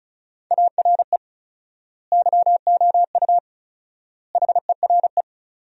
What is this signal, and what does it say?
Morse code, heard as a keyed tone, about 34 wpm, 700 Hz. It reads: ARE YOU HERE
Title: 4x Word-Spacing - Sentences from Top 400 Words - 35wpm